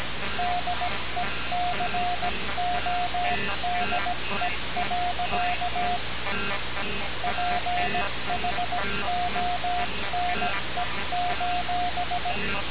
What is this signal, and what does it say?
Morse code, heard as a keyed tone, about 17 wpm, 710 Hz, with noise. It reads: DECODE. PHONE:
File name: radiocap / REC z